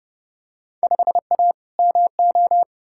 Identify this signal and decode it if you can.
Morse code, heard as a keyed tone, about 30 wpm, 700 Hz. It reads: 5A MO
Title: Field Day Exchanges - 30wpm